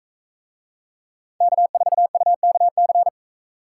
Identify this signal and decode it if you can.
Morse code, heard as a keyed tone, about 44 wpm, 700 Hz. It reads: K4UKC